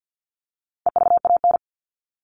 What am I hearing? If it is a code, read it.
Morse code, heard as a keyed tone, about 50 wpm, 700 Hz. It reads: EVAN
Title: All Cut Numbers - 4-Digit 50wpm